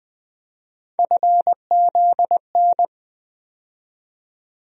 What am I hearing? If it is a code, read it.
Morse code, heard as a keyed tone, about 21 wpm, 700 Hz. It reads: FZN
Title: USA Call Sign Postfixes - 3 Letters 20wpm